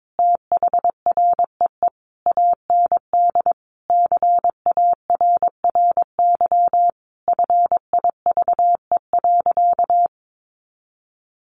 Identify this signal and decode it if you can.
Morse code, heard as a keyed tone, about 22 wpm, 700 Hz. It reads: THREE AND CARRY FIVE.
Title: Sentences from Top 400 Words 22wpm